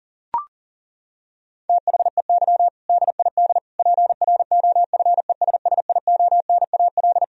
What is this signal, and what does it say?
Morse code, heard as a keyed tone, about 37 wpm, 700 Hz. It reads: THEY DID PROFESSIONAL
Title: Sentences from Top 1000 Words - Review Part 4 - 40wpm